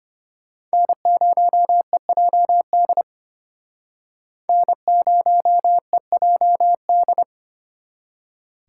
Morse code, heard as a keyed tone, about 33 wpm, 700 Hz. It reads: N0EJD N0EJD